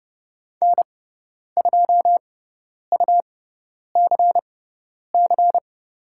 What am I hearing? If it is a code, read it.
Morse code, heard as a keyed tone, about 30 wpm, spaced as for 10 wpm, 700 Hz. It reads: N2UCC
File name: Lesson 17g - Letter u - Call Signs 30wpm